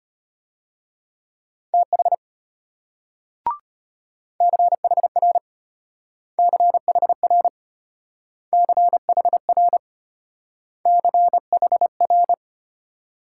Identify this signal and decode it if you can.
Morse code, heard as a keyed tone, about 32 wpm, 700 Hz. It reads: TH CHR CHR CHR CHR